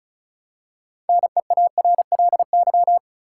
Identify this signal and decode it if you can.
Morse code, heard as a keyed tone, about 36 wpm, 700 Hz. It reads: NEARLY
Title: Six Letter Words - 35wpm